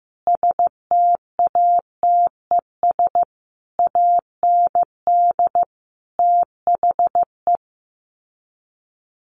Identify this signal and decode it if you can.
Morse code, heard as a keyed tone, about 14 wpm, 700 Hz. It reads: STATES AND THE
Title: Sets of 5 Words 15wpm